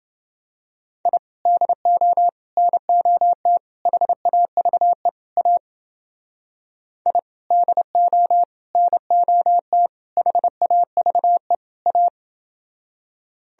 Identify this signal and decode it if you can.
Morse code, heard as a keyed tone, about 30 wpm, 700 Hz. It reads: I DO NOT HAVE A I DO NOT HAVE A